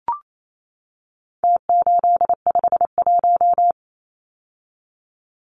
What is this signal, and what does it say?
Morse code, heard as a keyed tone, about 28 wpm, 700 Hz. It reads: T851
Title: Numbers - Contest Style - 4-Digit 28wpm